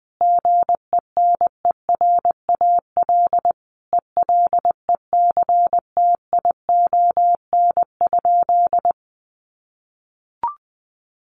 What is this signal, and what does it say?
Morse code, heard as a keyed tone, about 20 wpm, 700 Hz. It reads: GENERAL ELECTION?
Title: Sentences from Top 1500 Words 20wpm